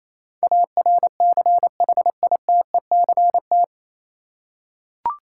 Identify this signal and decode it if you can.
Morse code, heard as a keyed tone, about 28 wpm, 700 Hz. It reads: ARCHITECT
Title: Practice QSO - Exchange 3 - More Info - 28wpm